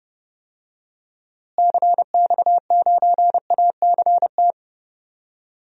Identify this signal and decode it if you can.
Morse code, heard as a keyed tone, about 30 wpm, 700 Hz. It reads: CX9ACT
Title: DXCC-100 Call Signs - 6 Characters - Repeated 2x - 30wpm